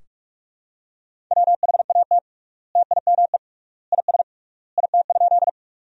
Morse code, heard as a keyed tone, about 46 wpm, 700 Hz. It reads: WHAT TIME IS IT?
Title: 3x Word-Spacing - Sentences from Top 100 Words - 45wpm